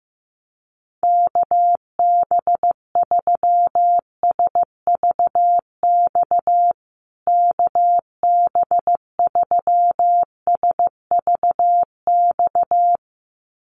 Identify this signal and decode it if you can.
Morse code, heard as a keyed tone, about 15 wpm, 700 Hz. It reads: KB3SVX KB3SVX
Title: USA Call Signs Repeated 2x 15wpm